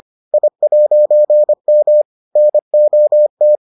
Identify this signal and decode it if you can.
Morse code, heard as a keyed tone, about 25 wpm, 600 Hz. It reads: I'M NOT